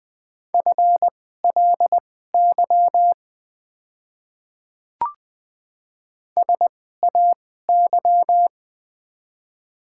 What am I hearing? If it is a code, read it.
Morse code, heard as a keyed tone, about 21 wpm, spaced as for 15 wpm, 700 Hz. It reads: FLY SAY
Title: 2x Character Spacing - Top 500 Words - 20wpm